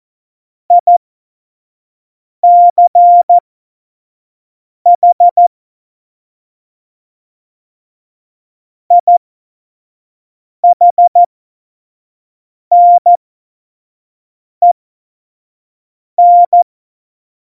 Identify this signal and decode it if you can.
Morse code, heard as a keyed tone, about 14 wpm, spaced as for 5 wpm, 700 Hz. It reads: ICH IHNEN
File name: Morsekurs / Witze / Witz 15. Dezember, F70